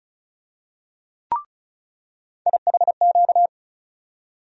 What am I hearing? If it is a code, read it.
Morse code, heard as a keyed tone, about 34 wpm, 700 Hz. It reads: IHQ
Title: USA Call Sign Postfixes - 3 Letters 35wpm